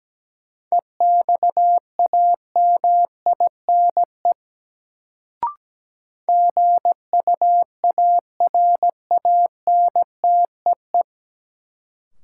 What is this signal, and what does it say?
Morse code, heard as a keyed tone, about 17 wpm, 700 Hz. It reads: EXAMINE GUARANTEE
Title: Sentences from Top 1600 Words - New Word Drill - 17wpm